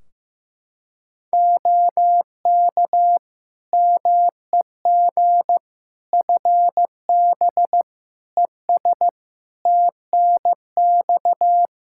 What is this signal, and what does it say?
Morse code, heard as a keyed tone, about 16 wpm, 700 Hz. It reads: OK MEG FB ES TNX
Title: Practice QSO - Exchange 2 - Info - 15wpm